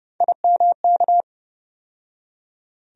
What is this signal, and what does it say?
Morse code, heard as a keyed tone, about 30 wpm, 700 Hz. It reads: IMK
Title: USA Call Sign Postfixes - 3 Letters 30wpm